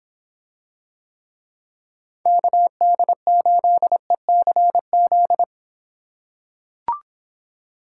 Morse code, heard as a keyed tone, about 24 wpm, 700 Hz. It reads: KD8ECZ